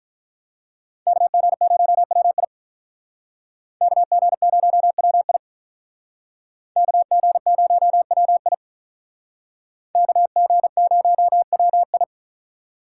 Morse code, heard as a keyed tone, about 46 wpm, 700 Hz. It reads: KG0WI KG0WI KG0WI KG0WI